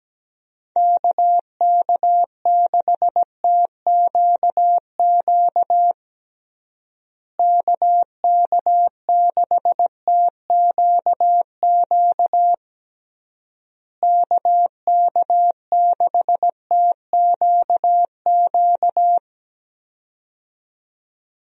Morse code, heard as a keyed tone, about 17 wpm, 700 Hz. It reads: KK6TQQ KK6TQQ KK6TQQ
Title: USA Call Signs - 6 Characters - Repeated 3x - 17wpm